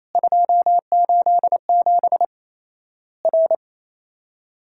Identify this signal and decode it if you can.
Morse code, heard as a keyed tone, about 28 wpm, 700 Hz. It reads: 287 R
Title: Worked All Europe DX Contest - QTC Exchange - 28wpm